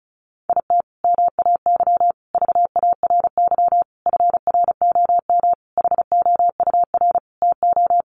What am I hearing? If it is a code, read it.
Morse code, heard as a keyed tone, about 34 wpm, 700 Hz. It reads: IT MAY VARY FROM HOUR TO
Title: Sentences from Top 1200 Words 35wpm